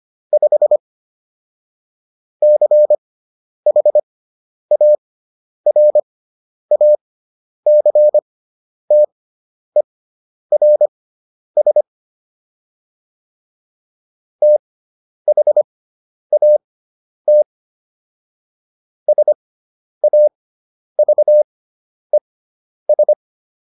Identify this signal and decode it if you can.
Morse code, heard as a keyed tone, about 25 wpm, spaced as for 10 wpm, 600 Hz. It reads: 5 CHARACTERS THAT SAVES